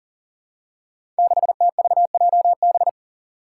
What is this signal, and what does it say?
Morse code, heard as a keyed tone, about 40 wpm, 700 Hz. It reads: 6TVJB
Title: Five Challenging Letter-Number Combinations 40wpm